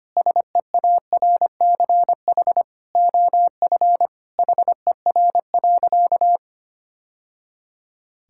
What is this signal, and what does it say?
Morse code, heard as a keyed tone, about 25 wpm, 700 Hz. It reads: SEARCH OF HER.
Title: Sentences from Top 1000 Words - Review Part 5 - 25wpm